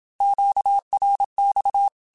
Morse code, heard as a keyed tone, about 27 wpm, 800 Hz. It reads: QRX
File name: QSOs / QSOs-mp3s / WA7QOX 2@25